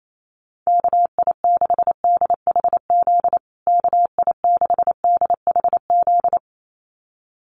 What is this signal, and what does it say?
Morse code, heard as a keyed tone, about 28 wpm, 700 Hz. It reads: KI6DHZ KI6DHZ
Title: USA Call Signs Repeated 2x 28wpm